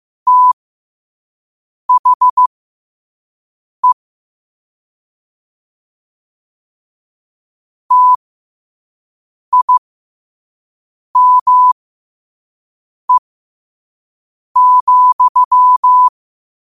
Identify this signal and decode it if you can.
Morse code, heard as a keyed tone, about 15 wpm, spaced as for 5 wpm, 1000 Hz. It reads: THE TIME,
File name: qotdfiles / 5wpm